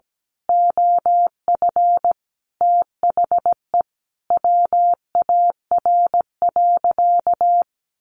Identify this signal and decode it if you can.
Morse code, heard as a keyed tone, about 17 wpm, 700 Hz. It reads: OF THE WAR.
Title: Sentences from Top 1400 Words 17wpm